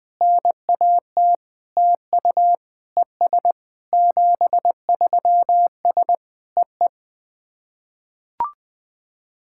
Morse code, heard as a keyed tone, about 21 wpm, 700 Hz. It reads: NAT TU ES 73S EE
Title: SOTA QSO Practice 20wpm